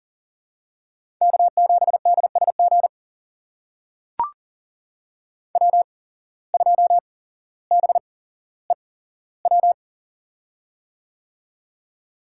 Morse code, heard as a keyed tone, about 40 wpm, spaced as for 17 wpm, 700 Hz. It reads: K7DSG W2BEW
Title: USA Call Signs - 5 Characters - 8x Character Spacing - 40wpm